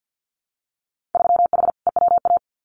Morse code, heard as a keyed tone, about 50 wpm, 700 Hz. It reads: 3H EWA
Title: Winter Field Day Exchanges Encore 50wpm